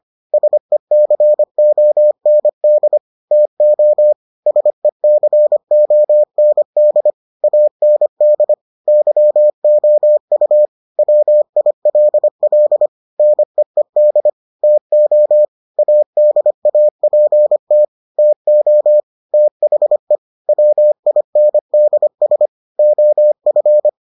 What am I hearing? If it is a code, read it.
Morse code, heard as a keyed tone, about 25 wpm, 600 Hz. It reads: SECOND TO SECOND AND YOU WILL NEED TO ADAPT TO THE WINDS OF